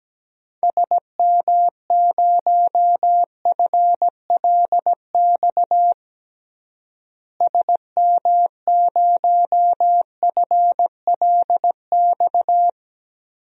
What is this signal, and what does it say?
Morse code, heard as a keyed tone, about 17 wpm, 700 Hz. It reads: SM0FLX SM0FLX
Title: DXCC-100 Call Signs - 6 Characters - Repeated 3x - 17wpm